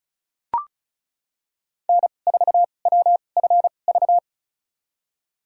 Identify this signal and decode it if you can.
Morse code, heard as a keyed tone, about 36 wpm, spaced as for 25 wpm, 700 Hz. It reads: N4WFV
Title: USA Call Signs - 2x Character Spacing - 35wpm